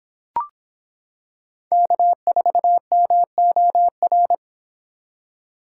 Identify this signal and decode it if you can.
Morse code, heard as a keyed tone, about 26 wpm, 700 Hz. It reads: K4MOR